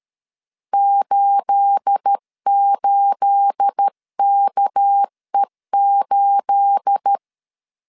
Morse code, heard as a keyed tone, about 13 wpm, 800 Hz. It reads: 88KE8
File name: LEZIONI / 9 28 12wpm 2351